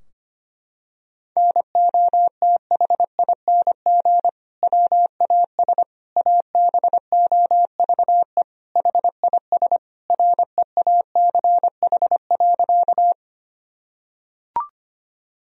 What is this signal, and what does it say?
Morse code, heard as a keyed tone, about 25 wpm, 700 Hz. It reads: NOTHING WAS ABOVE HIS REACH.
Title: Sentences from Top 1000 Words - Review Part 2 - 25wpm